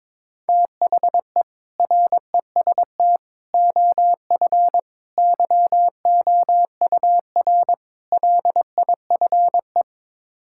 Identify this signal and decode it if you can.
Morse code, heard as a keyed tone, about 21 wpm, 700 Hz. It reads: THE REST OF YOUR LIFE
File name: practice / 22wpm / Build A Phrase - 5-Word Phrases - 22wpm